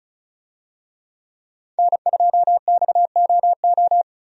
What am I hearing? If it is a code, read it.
Morse code, heard as a keyed tone, about 36 wpm, 700 Hz. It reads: N2XOO
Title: USA Call Signs - 5 Characters - 35wpm